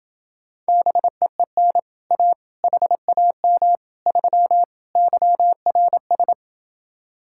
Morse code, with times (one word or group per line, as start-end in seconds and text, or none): 0.7-1.8 BEEN
2.1-2.3 A
2.6-3.8 HAM
4.1-4.6 3
5.0-6.3 YRS